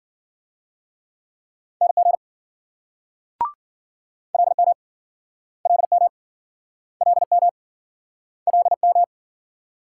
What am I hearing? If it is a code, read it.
Morse code, heard as a keyed tone, about 46 wpm, 700 Hz. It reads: NG LM LM LM LM
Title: Speed-Racing Multi-letter Phonemes - 40wpm